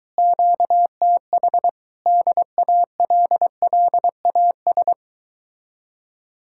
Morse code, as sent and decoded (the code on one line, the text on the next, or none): --.- - .... / -.. .- .-.. .-.. .- ...
QTH DALLAS